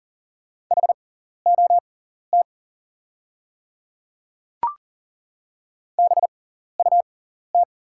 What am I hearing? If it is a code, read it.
Morse code, heard as a keyed tone, about 42 wpm, spaced as for 14 wpm, 690 Hz. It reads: HOT BUT
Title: 6x Character Spacing - Top 100 Words - 40wpm